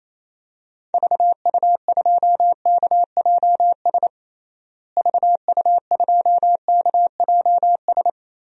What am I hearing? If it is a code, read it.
Morse code, heard as a keyed tone, about 28 wpm, 700 Hz. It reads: VU2KJS VU2KJS
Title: DXCC-100 Call Signs - 6 Characters - Repeated 3x - 28wpm